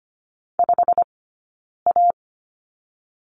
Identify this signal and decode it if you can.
Morse code, heard as a keyed tone, about 25 wpm, 700 Hz. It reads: 5 A
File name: Lesson 15d - Number 5 - Two Characters 25wpm